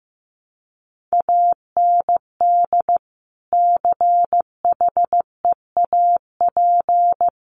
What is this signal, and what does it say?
Morse code, heard as a keyed tone, about 15 wpm, 700 Hz. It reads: AND CHEAP
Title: Sentences from Top 1200 Words 15wpm